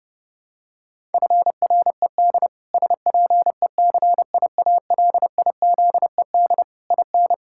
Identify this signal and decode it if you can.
Morse code, heard as a keyed tone, about 32 wpm, 700 Hz. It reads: FRED SPECIALIZED IN
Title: Pangram Challenge v2 - 30wpm